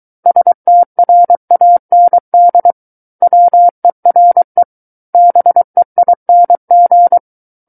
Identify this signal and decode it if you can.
Morse code, heard as a keyed tone, about 23 wpm, 700 Hz. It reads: STRAND WERE BEING
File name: Book One / 14-Chapter 14 - 23 WPM